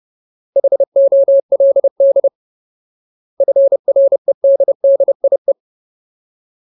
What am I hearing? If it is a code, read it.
Morse code, heard as a keyed tone, about 30 wpm, 550 Hz. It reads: HOLD FREDDIE
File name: Ultimate Word List / 30WPM Ultimate